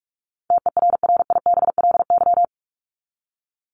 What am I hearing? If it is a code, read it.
Morse code, heard as a keyed tone, about 44 wpm, 700 Hz. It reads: TERRIBLY
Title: Top 1751-2000 Words 45wpm